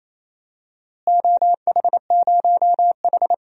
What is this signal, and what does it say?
Morse code, heard as a keyed tone, about 28 wpm, 700 Hz. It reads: OH0H
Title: DXCC-100 Call Signs - 4 Characters - Repeated 2x - 28wpm